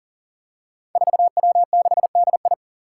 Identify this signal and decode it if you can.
Morse code, heard as a keyed tone, about 42 wpm, 700 Hz. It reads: 4W6DI
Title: Five Challenging Letter-Number Combinations 40wpm